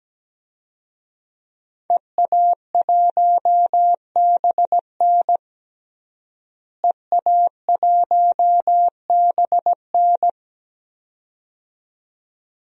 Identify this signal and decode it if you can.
Morse code, heard as a keyed tone, about 17 wpm, 700 Hz. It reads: EA1BN EA1BN
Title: DXCC-100 Call Signs - 5 Characters - Repeated 2x - 17wpm